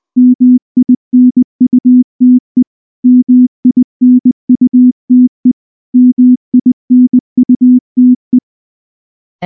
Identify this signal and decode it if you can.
Morse code, heard as a keyed tone, about 19 wpm, 250 Hz. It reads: MINUTE MINUTE MINUTE
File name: morse-training-20WPM